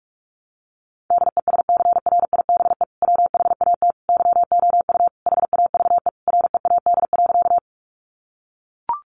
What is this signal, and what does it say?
Morse code, heard as a keyed tone, about 44 wpm, 700 Hz. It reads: DESCRIBE WHAT YOU HAVE READ.
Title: Sentences from Top 1000 Words 45wpm